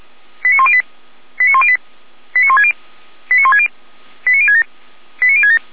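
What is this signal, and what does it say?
Five-tone selective call (ZVEI), heard as quick runs of five tones, tones of about 70 ms. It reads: 89108 89108 89160 89160 89067 89067
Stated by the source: ZVEI five-tone call received by SDR, from the Feuerwehr Orlamünde server.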